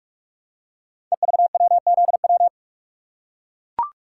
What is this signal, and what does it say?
Morse code, heard as a keyed tone, about 42 wpm, 700 Hz. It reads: EVWZW
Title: Code Groups - 5 Letter:Numbers - 45wpm